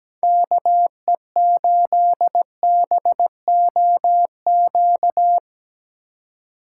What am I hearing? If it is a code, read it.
Morse code, heard as a keyed tone, about 19 wpm, 700 Hz. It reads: KE8BOQ
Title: Speed Racing - USA 2x3 Call Signs 17wpm